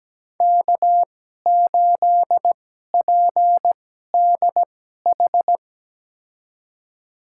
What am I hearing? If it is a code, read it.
Morse code, heard as a keyed tone, about 17 wpm, spaced as for 12 wpm, 700 Hz. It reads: K8PDH
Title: USA Call Signs - 5 Characters - 2x Character Spacing - 17wpm